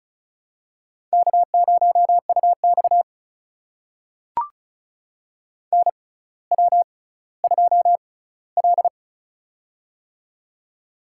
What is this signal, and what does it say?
Morse code, heard as a keyed tone, about 34 wpm, spaced as for 18 wpm, 700 Hz. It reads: K0UX NW2L
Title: USA Call Signs - 4 Characters - 6x Character Spacing - 35wpm